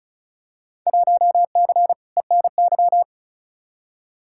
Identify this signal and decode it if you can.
Morse code, heard as a keyed tone, about 36 wpm, 700 Hz. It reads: 1C ENY
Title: Field Day Exchanges Encore - 35wpm